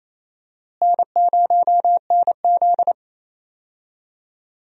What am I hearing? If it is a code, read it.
Morse code, heard as a keyed tone, about 28 wpm, 700 Hz. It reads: N0NZ
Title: USA Call Signs - 4 Characters - Repeated 2x - 28wpm